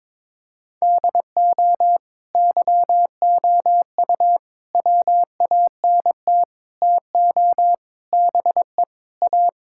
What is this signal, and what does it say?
Morse code, heard as a keyed tone, about 22 wpm, 700 Hz. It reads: DO YOU WANT TO BE A
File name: practice / 22wpm / Sentences with 4-Word Phrases - 22wpm